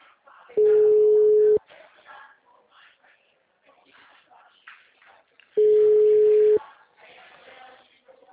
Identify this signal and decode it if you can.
Ringback tone (Europe, 425 Hz), with speech.
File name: out- -2005- - - .64141